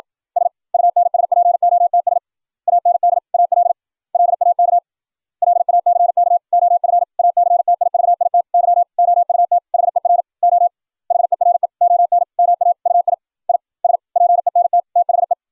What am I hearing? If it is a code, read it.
Morse code, heard as a keyed tone, about 56 wpm, 700 Hz. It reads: SUNSPOTS AND AL LAK LACK OF ACTIVITY OUT HER O HERE ON MAUI I SPENT THE